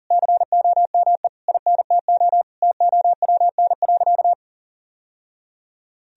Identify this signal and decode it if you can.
Morse code, heard as a keyed tone, about 40 wpm, 700 Hz. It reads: COME INTO TOWN.
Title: Sentences from Top 1000 Words - Review Part 2 - 40wpm